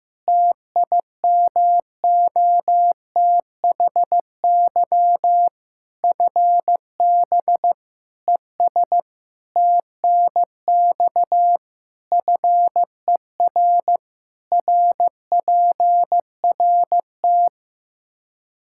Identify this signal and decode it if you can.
Morse code, heard as a keyed tone, about 16 wpm, 700 Hz. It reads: TIMOTHY FB ES TNX FER RPRT
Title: K1USN SST QSO Practice 15wpm